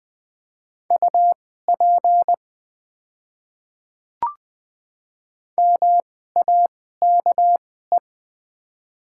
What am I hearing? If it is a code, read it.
Morse code, heard as a keyed tone, about 20 wpm, spaced as for 15 wpm, 690 Hz. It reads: UP MAKE
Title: 2x Character Spacing - Top 100 Words - 20wpm